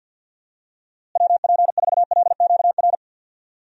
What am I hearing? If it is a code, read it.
Morse code, heard as a keyed tone, about 50 wpm, 700 Hz. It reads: WP4LQR
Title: USA Call Signs - 6 Characters - Repeated 3x - 50wpm